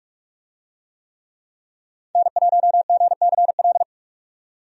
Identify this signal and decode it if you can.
Morse code, heard as a keyed tone, about 46 wpm, 700 Hz. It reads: N1GCL
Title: USA Call Signs - 5 Characters - 45wpm